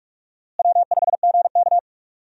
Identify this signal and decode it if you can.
Morse code, heard as a keyed tone, about 45 wpm, 700 Hz. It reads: W5GK
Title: USA Call Signs - 4 Characters - Repeated 3x - 45wpm